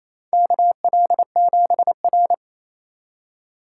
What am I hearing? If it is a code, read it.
Morse code, heard as a keyed tone, about 27 wpm, 700 Hz. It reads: KL7R
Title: USA Call Signs - 4x Character Spacing - 28wpm